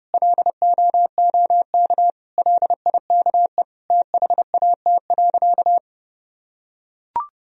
Morse code, heard as a keyed tone, about 32 wpm, 700 Hz. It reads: LOOK LIKE THAT.
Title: Sentences from Top 1000 Words - Review Part 1 - 30wpm